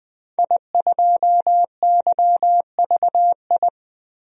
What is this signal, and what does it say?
Morse code, heard as a keyed tone, about 20 wpm, 700 Hz. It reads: I2YVI
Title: DXCC-100 Call Signs - 5 Characters - Repeated 3x - 20wpm